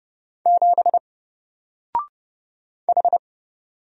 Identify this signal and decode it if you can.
Morse code, heard as a keyed tone, about 30 wpm, 700 Hz. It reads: 7 H
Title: Single Letter-Number Alt 30wpm